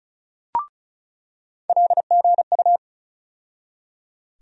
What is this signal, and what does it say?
Morse code, heard as a keyed tone, about 36 wpm, 700 Hz. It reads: LGU
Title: USA Call Sign Postfixes - 3 Letters 35wpm